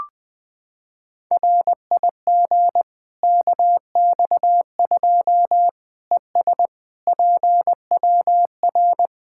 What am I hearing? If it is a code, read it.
Morse code, heard as a keyed tone, about 20 wpm, 700 Hz. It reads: RIG KX2 ES PWR
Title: K1USN SST QSO Practice 20wpm